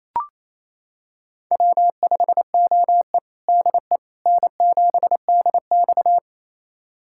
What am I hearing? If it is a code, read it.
Morse code, heard as a keyed tone, about 27 wpm, 700 Hz. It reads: W5OE DE N7DX
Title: USA Callsign DE Callsign - 4 Characters - 28wpm